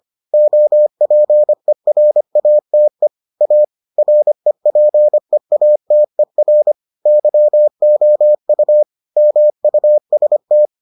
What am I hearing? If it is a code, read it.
Morse code, heard as a keyed tone, about 25 wpm, 600 Hz. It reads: OPERATE A REPEATER YOU MUST